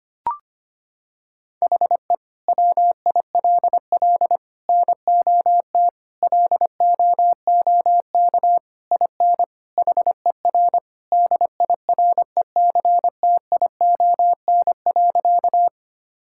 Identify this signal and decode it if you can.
Morse code, heard as a keyed tone, about 25 wpm, 700 Hz. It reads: HE WILL NOT LOOK IN HER DIRECTION.
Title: Sentences from Top 1000 Words - Review Part 5 - 25wpm